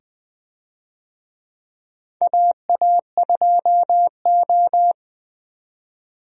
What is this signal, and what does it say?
Morse code, heard as a keyed tone, about 20 wpm, 700 Hz. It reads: AA2O
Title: USA Call Signs - 4 Characters - 20wpm